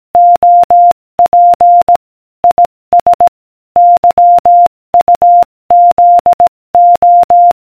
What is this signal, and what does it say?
Morse code, heard as a keyed tone, about 18 wpm, 700 Hz. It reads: OP IS YUZO